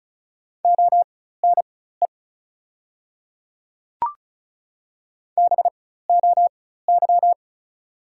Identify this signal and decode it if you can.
Morse code, heard as a keyed tone, about 36 wpm, spaced as for 16 wpm, 700 Hz. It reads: ONE BOY
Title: 4x Character Spacing - Top 300 Words - 35wpm